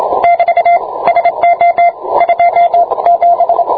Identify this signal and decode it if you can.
Morse code, heard as a keyed tone, about 26 wpm, 690 Hz, with radio noise. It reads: <BT> SO 27